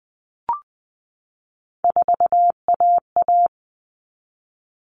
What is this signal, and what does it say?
Morse code, heard as a keyed tone, about 20 wpm, 700 Hz. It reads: 4AA
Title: All Cut Numbers - 3-Digit 20wpm